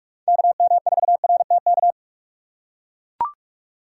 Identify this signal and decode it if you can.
Morse code, heard as a keyed tone, about 42 wpm, 700 Hz. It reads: KM4RTK
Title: USA Call Signs - 6 Characters - 3x Character Spacing - 45wpm